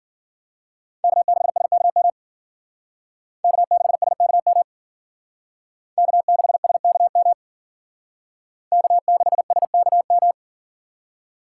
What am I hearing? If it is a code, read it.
Morse code, heard as a keyed tone, about 51 wpm, 700 Hz. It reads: K6SKM K6SKM K6SKM K6SKM